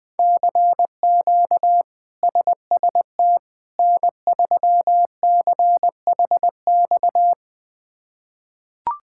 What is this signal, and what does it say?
Morse code, heard as a keyed tone, about 20 wpm, 700 Hz. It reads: CQ SST N3CHX
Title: K1USN SST QSO Practice - 20wpm